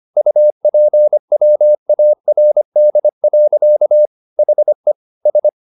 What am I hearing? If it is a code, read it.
Morse code, heard as a keyed tone, about 25 wpm, 600 Hz. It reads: UPWARD. HE S
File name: the old man and the sea part 1 25wpm